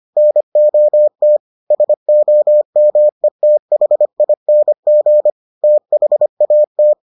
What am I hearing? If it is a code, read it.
Morse code, heard as a keyed tone, about 25 wpm, 600 Hz. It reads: NOT SOMETHING THAT